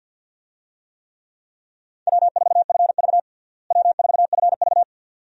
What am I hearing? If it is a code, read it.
Morse code, heard as a keyed tone, about 52 wpm, 690 Hz. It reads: W4FV W4FV
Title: USA Call Signs - 4 Characters - Repeated 2x - 50wpm